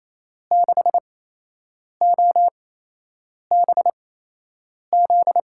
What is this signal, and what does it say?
Morse code, heard as a keyed tone, about 28 wpm, spaced as for 8 wpm, 700 Hz. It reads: 6OBZ
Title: USA Call Signs - 6 Characters - 8x Character Spacing - 28wpm